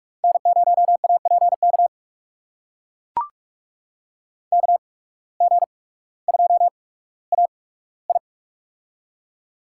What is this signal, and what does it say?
Morse code, heard as a keyed tone, about 44 wpm, spaced as for 19 wpm, 700 Hz. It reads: N0APK KG2AI
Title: USA Call Signs - 5 Characters - 8x Character Spacing - 45wpm